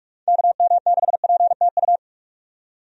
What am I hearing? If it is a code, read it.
Morse code, heard as a keyed tone, about 43 wpm, 700 Hz. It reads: KM6PTU
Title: USA Call Signs - 6 Characters - 45wpm